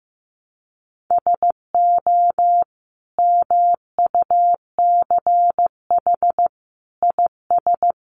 Morse code, heard as a keyed tone, about 15 wpm, 700 Hz. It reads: SO MUCH IS
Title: Sentences from Top 1200 Words 15wpm